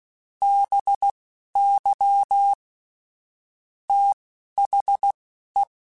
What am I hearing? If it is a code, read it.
Morse code, heard as a keyed tone, about 16 wpm, spaced as for 11 wpm, 800 Hz. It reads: BY THE
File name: Technical Subjects / Technical Subjects-mp3s / XC @10